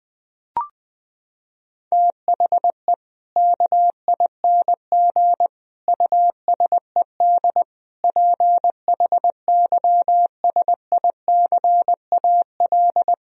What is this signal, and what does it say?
Morse code, heard as a keyed tone, about 20 wpm, 700 Hz. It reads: THE KING USED PHYSICAL
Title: Sentences from Top 1000 Words - Review Part 4 - 20wpm